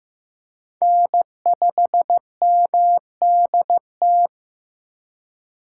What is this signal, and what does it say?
Morse code, heard as a keyed tone, about 15 wpm, 700 Hz. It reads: N5MDT
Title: USA Call Signs - 5 Characters - 8x Character Spacing - 15wpm